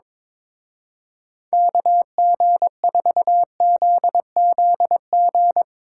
Morse code, heard as a keyed tone, about 24 wpm, 700 Hz. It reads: KG4ZZG
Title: Speed Racing - USA 2x3 Call Signs 22wpm